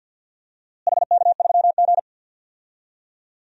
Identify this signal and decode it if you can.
Morse code, heard as a keyed tone, about 50 wpm, 700 Hz. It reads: HK3G
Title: DXCC-100 Call Signs - 4 Characters - Repeated 3x - 50wpm